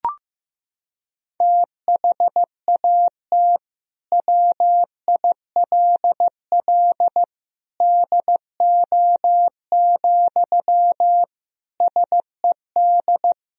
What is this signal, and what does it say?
Morse code, heard as a keyed tone, about 15 wpm, 700 Hz. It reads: THAT WILL DO, SED
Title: Sentences from Top 100 Words and Abbreviations 15wpm